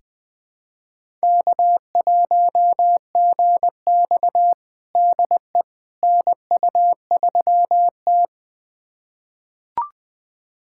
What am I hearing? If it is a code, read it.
Morse code, heard as a keyed tone, about 21 wpm, 700 Hz. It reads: K1GX DE NU3T
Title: USA Callsign DE Callsign - 4 Characters - 20wpm